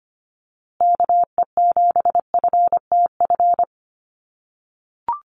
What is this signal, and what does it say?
Morse code, heard as a keyed tone, about 25 wpm, 700 Hz. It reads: KE7FTF
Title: USA Call Signs Repeated 2x 25wpm